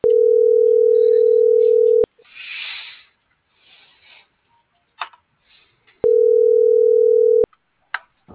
Ringback tone (North America), 440 + 480 Hz.